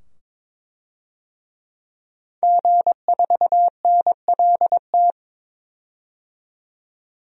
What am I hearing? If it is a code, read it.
Morse code, heard as a keyed tone, about 22 wpm, 700 Hz. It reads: G4NLT
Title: DXCC-100 Call Signs - 5 Characters - 22wpm